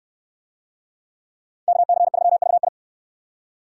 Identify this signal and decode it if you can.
Morse code, heard as a keyed tone, about 69 wpm, 700 Hz. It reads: KC3VI